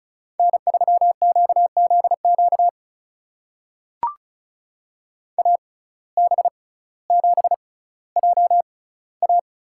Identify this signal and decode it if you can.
Morse code, heard as a keyed tone, about 34 wpm, spaced as for 18 wpm, 700 Hz. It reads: N3QZQ AB7JA
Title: USA Call Signs - 5 Characters - 6x Character Spacing - 35wpm